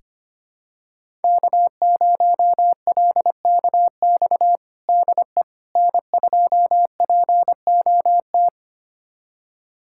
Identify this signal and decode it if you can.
Morse code, heard as a keyed tone, about 25 wpm, 700 Hz. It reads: K0LKX DE N2POT
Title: USA Callsign DE Callsign - 5 Characters - 25wpm